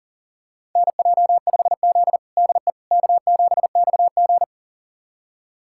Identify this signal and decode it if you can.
Morse code, heard as a keyed tone, about 37 wpm, 700 Hz. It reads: NJ5Z DE K7XG
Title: USA Callsign DE Callsign - 4 Characters - 40wpm